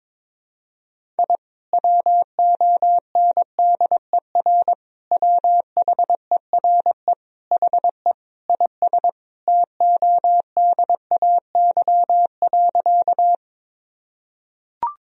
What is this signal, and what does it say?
Morse code, heard as a keyed tone, about 22 wpm, 700 Hz. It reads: I WONDER WHERE HE IS TODAY.
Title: Sentences from Top 1000 Words - Review Part 3 - 22wpm